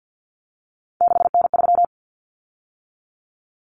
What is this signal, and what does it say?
Morse code, heard as a keyed tone, about 50 wpm, 700 Hz. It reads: 6N3
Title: Common Cut Numbers - 3-Digit 50wpm